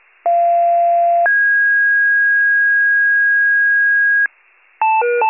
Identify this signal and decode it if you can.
Two-tone page, heard as a one-second tone, then a three-second tone, about 690 then 1650 Hz, with noise. It is heard from 0.3 to 4.3 s.